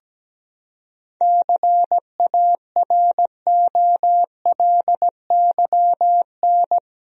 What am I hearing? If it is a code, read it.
Morse code, heard as a keyed tone, about 17 wpm, 700 Hz. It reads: CAROLYN
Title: USA Common Names - 17wpm